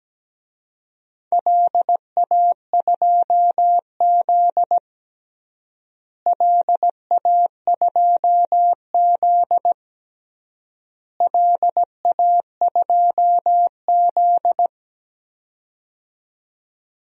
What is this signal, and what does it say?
Morse code, heard as a keyed tone, about 17 wpm, 700 Hz. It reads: LA2Z LA2Z LA2Z
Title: DXCC-100 Call Signs - 4 Characters - Repeated 3x - 17wpm